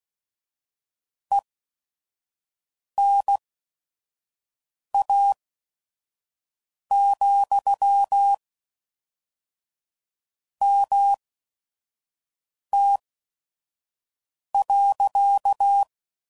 Morse code, heard as a keyed tone, about 16 wpm, spaced as for 5 wpm, 800 Hz. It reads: ENA, MT.
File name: CW Code Groups @5WPM / QSOs / WD0FBT 1